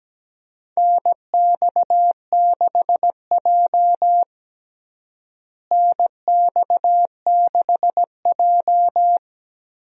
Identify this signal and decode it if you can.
Morse code, heard as a keyed tone, about 17 wpm, 700 Hz. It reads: NX6J NX6J
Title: USA Call Signs - 4 Characters - Repeated 3x - 17wpm